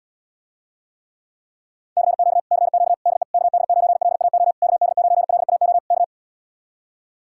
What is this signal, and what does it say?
Morse code, heard as a keyed tone, about 77 wpm, 700 Hz. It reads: CQ CQ DE KD9LAO KD9LAO K